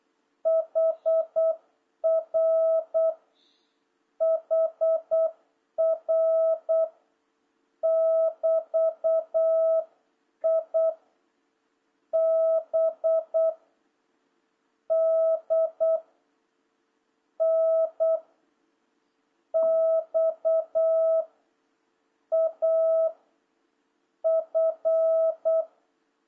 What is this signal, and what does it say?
Morse code, heard as a keyed tone, about 8 wpm, 650 Hz. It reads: HR HR <BT> I B D N X A F
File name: jh8cbh / 2026cw